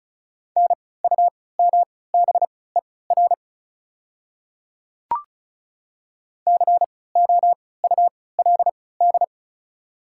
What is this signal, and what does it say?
Morse code, heard as a keyed tone, about 36 wpm, spaced as for 20 wpm, 690 Hz. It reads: NUMBER COULD
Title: 3x Character Spacing - Top 200 Words - 35wpm